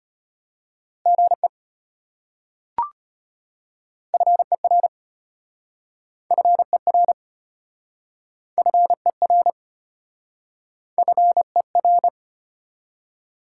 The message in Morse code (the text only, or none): GE FER FER FER FER